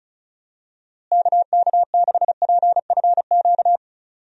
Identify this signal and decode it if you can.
Morse code, heard as a keyed tone, about 36 wpm, 700 Hz. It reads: KK6PFQ
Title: USA Call Signs - 6 Characters - Repeated 3x - 35wpm